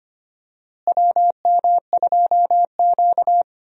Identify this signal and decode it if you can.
Morse code, heard as a keyed tone, about 25 wpm, 700 Hz. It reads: WM2Q
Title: USA Call Signs - 4 Characters - Repeated 2x - 25wpm